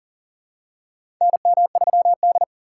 Morse code, heard as a keyed tone, about 40 wpm, 700 Hz. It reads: NM3D
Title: USA Call Signs - 4 Characters - 3x Character Spacing - 40wpm